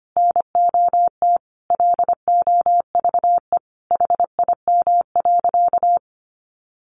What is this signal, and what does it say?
Morse code, heard as a keyed tone, about 25 wpm, 700 Hz. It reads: NOT LOVE HIM.
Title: Sentences from Top 600 Words 25wpm